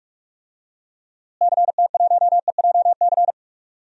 Morse code, heard as a keyed tone, about 41 wpm, 700 Hz. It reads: CT1EJC
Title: DXCC-100 Call Signs - 6 Characters - Repeated 2x - 45wpm